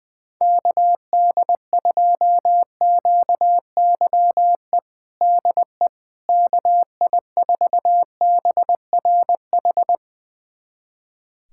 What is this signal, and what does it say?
Morse code, heard as a keyed tone, about 20 wpm, 700 Hz. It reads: KD2QYE DE KI4BRH
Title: USA Callsign DE Callsign 20wpm